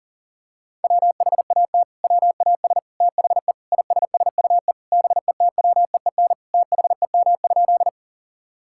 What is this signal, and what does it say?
Morse code, heard as a keyed tone, about 40 wpm, 700 Hz. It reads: WHAT WAS THE ISSUE BETWEEN THEM?
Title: Sentences from Top 1000 Words - Review Part 6 - 40wpm